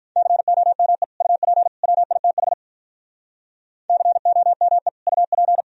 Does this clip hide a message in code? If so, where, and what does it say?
Yes: Morse code from 0.2 to 5.6 s, heard as a keyed tone, about 45 wpm, 700 Hz. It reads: COME UP WITH COME UP